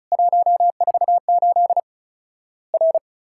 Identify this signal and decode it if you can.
Morse code, heard as a keyed tone, about 36 wpm, 700 Hz. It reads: 148 R
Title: Worked All Europe DX Contest - QTC Exchange - 35wpm